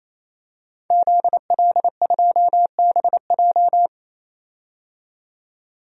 Morse code, heard as a keyed tone, about 28 wpm, 700 Hz. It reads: ZL2BJ
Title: DXCC-100 Call Signs - 5 Characters - 28wpm